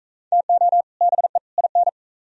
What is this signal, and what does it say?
Morse code, heard as a keyed tone, about 35 wpm, 700 Hz. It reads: TO BE IN
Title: Speed-Racing - Sets of 3 Words 28wpm